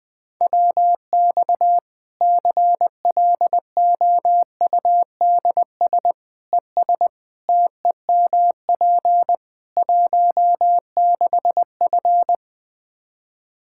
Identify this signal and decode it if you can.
Morse code, heard as a keyed tone, about 20 wpm, 700 Hz. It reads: WX CLOUDS ES TEMP 16F
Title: Practice QSO - Exchange 2 - Info - 20wpm